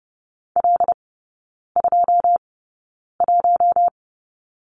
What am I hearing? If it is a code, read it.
Morse code, heard as a keyed tone, about 30 wpm, spaced as for 9 wpm, 700 Hz. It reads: L21
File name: 081 Lesson 14e - Number 2 - Three Characters 30wpm